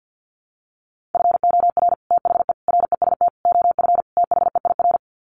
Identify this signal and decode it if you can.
Morse code, heard as a keyed tone, about 50 wpm, 700 Hz. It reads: FOR THE REST OF THEIR
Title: Sets of 5 Words Encore2 50wpm